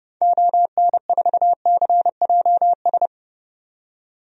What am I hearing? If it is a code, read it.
Morse code, heard as a keyed tone, about 30 wpm, 700 Hz. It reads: ON4CJS
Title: DXCC-100 Call Signs - 6 Characters - 30wpm